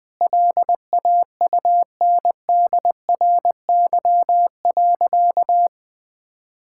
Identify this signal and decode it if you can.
Morse code, heard as a keyed tone, about 20 wpm, 700 Hz. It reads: LAUNDRY.
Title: Easy Sentences with Hints - 20wpm